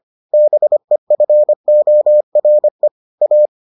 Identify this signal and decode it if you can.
Morse code, heard as a keyed tone, about 25 wpm, 600 Hz. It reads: BEFORE A